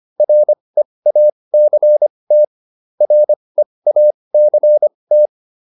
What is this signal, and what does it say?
Morse code, heard as a keyed tone, about 25 wpm, spaced as for 20 wpm, 600 Hz. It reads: REACT REACT